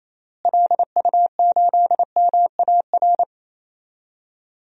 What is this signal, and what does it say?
Morse code, heard as a keyed tone, about 28 wpm, 700 Hz. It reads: LU8MAR
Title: DXCC-100 Call Signs - 6 Characters - Repeated 2x - 28wpm